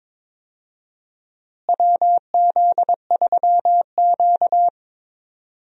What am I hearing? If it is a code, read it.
Morse code, heard as a keyed tone, about 22 wpm, 700 Hz. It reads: WZ3Q
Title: USA Call Signs - 4 Characters - Repeated 2x - 22wpm